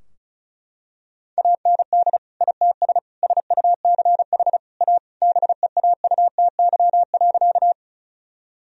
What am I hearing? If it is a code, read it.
Morse code, heard as a keyed tone, about 36 wpm, 700 Hz. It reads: AND ITS SUCH A BEAUTY.
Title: Sentences from Top 1000 Words - Review Part 2 - 35wpm